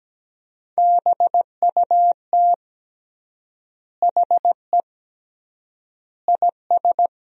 Morse code, heard as a keyed tone, about 17 wpm, 700 Hz. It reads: BUT HE IS
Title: 3x Word-Spacing - Sentences from Top 200 Words - 17wpm